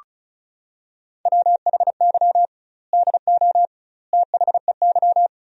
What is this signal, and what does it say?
Morse code, heard as a keyed tone, about 36 wpm, 700 Hz. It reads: WHY DO THEY
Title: 2x Word-Spacing - Sentences from Top 400 Words - 35wpm